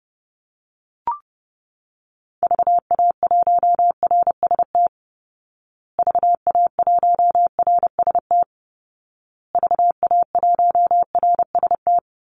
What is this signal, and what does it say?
Morse code, heard as a keyed tone, about 30 wpm, 700 Hz. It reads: VA1RST VA1RST VA1RST
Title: Call Signs Repeated 2x - 30wpm